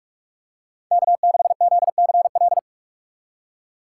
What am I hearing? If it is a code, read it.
Morse code, heard as a keyed tone, about 46 wpm, 700 Hz. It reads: K6ZCL
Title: K1USN SST QSO Practice - 45wpm